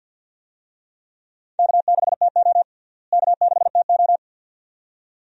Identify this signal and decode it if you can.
Morse code, heard as a keyed tone, about 51 wpm, 700 Hz. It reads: K6TO K6TO
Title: USA Call Signs - 4 Characters - Repeated 2x - 50wpm